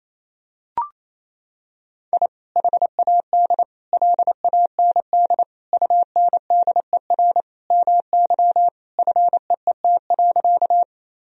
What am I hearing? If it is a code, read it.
Morse code, heard as a keyed tone, about 28 wpm, 700 Hz. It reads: I HAD LAND UNDER MY FEET.
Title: Sentences from Top 1000 Words - Review Part 1 - 28wpm